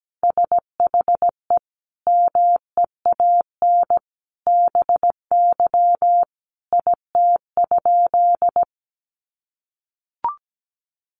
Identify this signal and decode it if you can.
Morse code, heard as a keyed tone, about 17 wpm, 700 Hz. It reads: SHE MEAN BY IT?
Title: Sentences from Top 200 Words 17wpm